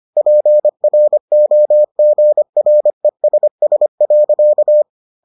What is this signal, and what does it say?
Morse code, heard as a keyed tone, about 25 wpm, 600 Hz. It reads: PROGRESS.